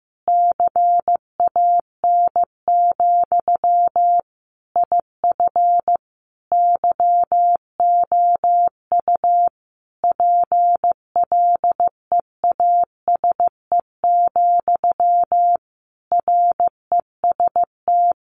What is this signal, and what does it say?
Morse code, heard as a keyed tone, about 15 wpm, 700 Hz. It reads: CAN, IF YOU PLEASE, REST
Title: Sentences from Top 1500 Words 15wpm